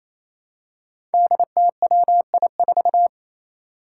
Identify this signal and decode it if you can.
Morse code, heard as a keyed tone, about 28 wpm, 700 Hz. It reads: DTWI4
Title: Five Challenging Letter-Number Combinations 28wpm